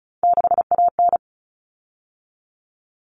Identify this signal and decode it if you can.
Morse code, heard as a keyed tone, about 34 wpm, 700 Hz. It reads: 6AN
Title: Common Cut Numbers - 3-Digit 35wpm